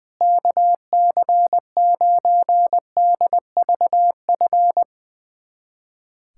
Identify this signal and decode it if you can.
Morse code, heard as a keyed tone, about 20 wpm, 700 Hz. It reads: KC9DVF
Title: USA Callsign DE Callsign - 6 Characters - 20wpm